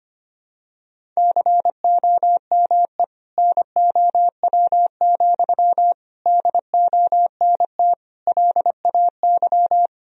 Morse code, heard as a keyed tone, about 25 wpm, 700 Hz. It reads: COME NOW, DONT LAY
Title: Sentences from Top 1000 Words - Review Part 2 - 25wpm